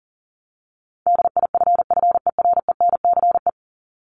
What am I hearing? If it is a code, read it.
Morse code, heard as a keyed tone, about 40 wpm, 700 Hz. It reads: DIFFERENCE
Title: ES:S Suffix - Top 100 Words with Base Words 40wpm